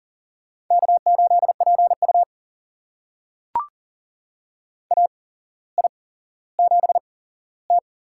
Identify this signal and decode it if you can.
Morse code, heard as a keyed tone, about 40 wpm, spaced as for 17 wpm, 700 Hz. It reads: K8PU AI7T
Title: USA Call Signs - 4 Characters - 8x Character Spacing - 40wpm